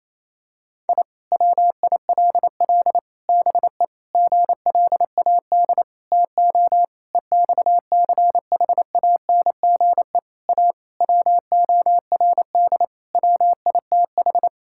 Morse code, heard as a keyed tone, about 28 wpm, 700 Hz. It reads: I WILL BE GLAD TO EXCHANGE A WORD WITH
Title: Sentences from Top 1000 Words - Review Part 6 - 28wpm